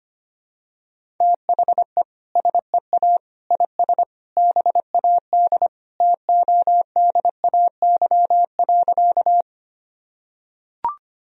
Morse code, heard as a keyed tone, about 25 wpm, 700 Hz. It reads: THE SEA IS BAD TODAY.
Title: Sentences from Top 1000 Words - Review Part 1 - 25wpm